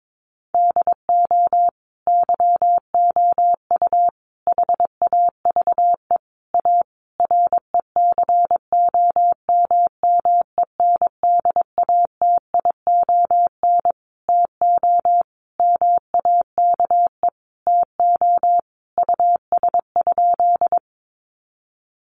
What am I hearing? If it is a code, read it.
Morse code, heard as a keyed tone, about 22 wpm, 700 Hz. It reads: DO YOU HAVE A RECOMMENDATION TO MAKE TO US?
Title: Sentences from Top 1400 Words 22wpm